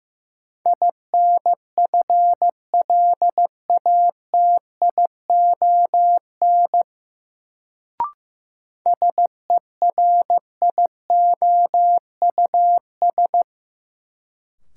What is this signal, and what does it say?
Morse code, heard as a keyed tone, about 15 wpm, 700 Hz. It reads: INFLATION SERIOUS
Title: Sentences from Top 1100 Words - New Word Drill - 15wpm